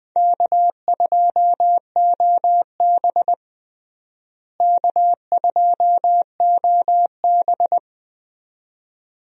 Morse code, heard as a keyed tone, about 20 wpm, 700 Hz. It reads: K2OB K2OB
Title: USA Call Signs - 4 Characters - Repeated 2x - 20wpm